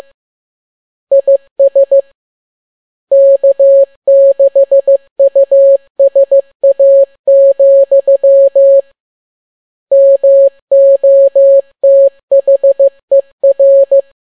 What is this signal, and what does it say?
Morse code, heard as a keyed tone, about 15 wpm, 550 Hz. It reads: IS K6USA, MOTHER